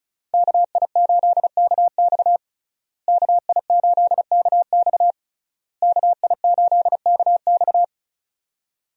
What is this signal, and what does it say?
Morse code, heard as a keyed tone, about 34 wpm, 700 Hz. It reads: KI8KX KI8KX KI8KX
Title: USA Call Signs - 5 Characters - Repeated 3x - 35wpm